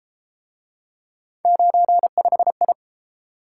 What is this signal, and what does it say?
Morse code, heard as a keyed tone, about 31 wpm, 700 Hz. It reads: 95I